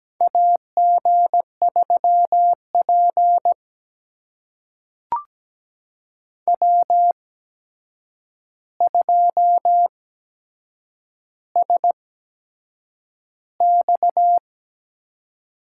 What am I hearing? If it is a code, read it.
Morse code, heard as a keyed tone, about 17 wpm, spaced as for 7 wpm, 700 Hz. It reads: AG3P W2SX
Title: USA Call Signs - 4 Characters - 8x Character Spacing - 17wpm